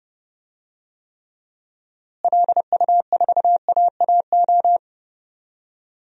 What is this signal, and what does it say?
Morse code, heard as a keyed tone, about 32 wpm, 700 Hz. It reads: LU4AAO
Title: DXCC-100 Call Signs - 6 Characters - Repeated 3x - 30wpm